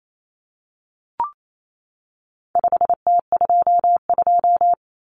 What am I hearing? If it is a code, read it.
Morse code, heard as a keyed tone, about 28 wpm, 700 Hz. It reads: 5T22
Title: Common Cut Numbers - 4-Digit 28wpm